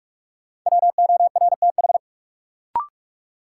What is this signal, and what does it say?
Morse code, heard as a keyed tone, about 46 wpm, 700 Hz. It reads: WORTH
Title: Sentences from Top 800 Words - New Word Drill - 45wpm